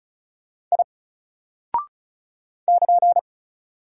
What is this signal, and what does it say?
Morse code, heard as a keyed tone, about 36 wpm, 700 Hz. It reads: I <KN>
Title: Single Character (Letter:Number:Punctuation:Prosign) Alt 35wpm